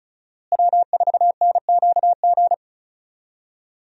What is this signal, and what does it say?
Morse code, heard as a keyed tone, about 36 wpm, 700 Hz. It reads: W4NQG
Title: USA Call Signs - 5 Characters - Repeated 2x - 35wpm